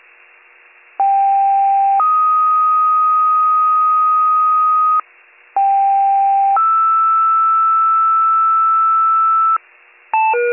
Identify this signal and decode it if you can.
Two-tone page, heard as a one-second tone, then a three-second tone, about 790 then 1250 Hz, about 790 then 1320 Hz. It is heard from 1.0 to 9.6 s.